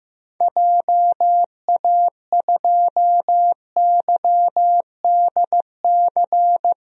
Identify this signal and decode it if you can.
Morse code, heard as a keyed tone, about 15 wpm, 700 Hz. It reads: JA2YDC
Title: DXCC-100 Call Signs - 6 Characters - Repeated 3x - 15wpm